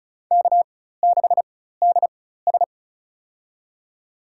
Morse code, heard as a keyed tone, about 36 wpm, spaced as for 16 wpm, 700 Hz. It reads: K6DS
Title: USA Call Signs - 4 Characters - 4x Character Spacing - 35wpm